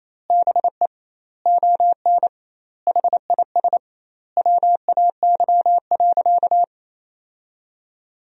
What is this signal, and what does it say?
Morse code, heard as a keyed tone, about 28 wpm, 700 Hz. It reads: BE ON HIS WAY.
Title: 2x Word-Spacing - Sentences from Top 300 Words - 28wpm